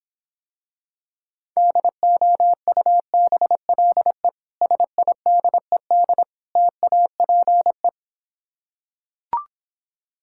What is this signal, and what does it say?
Morse code, heard as a keyed tone, about 26 wpm, 700 Hz. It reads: DOUBLE SIDED TAPE